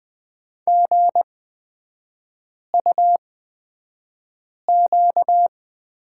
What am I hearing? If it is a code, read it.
Morse code, heard as a keyed tone, about 20 wpm, spaced as for 5 wpm, 690 Hz. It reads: GUQ
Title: 236 Lesson 38e - Letter q - Three Characters 20wpm